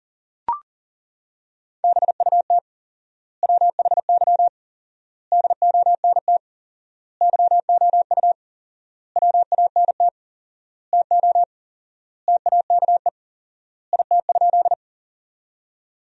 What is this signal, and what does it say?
Morse code, heard as a keyed tone, about 40 wpm, 700 Hz. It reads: BUT WHY DONT YOU WANT TO TAKE IT?
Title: 4x Word-Spacing - Sentences from Top 300 Words - 40wpm